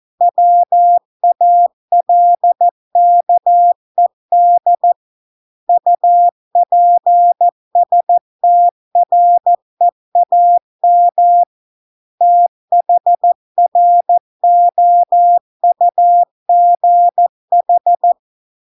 Morse code, heard as a keyed tone, about 14 wpm, 700 Hz. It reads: WALKED UPSTREAM THROUGH